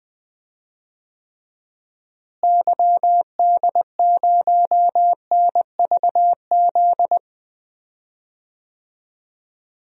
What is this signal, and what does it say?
Morse code, heard as a keyed tone, about 20 wpm, 700 Hz. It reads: YD0NVZ
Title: DXCC-100 Call Signs - 6 Characters - Repeated 3x - 20wpm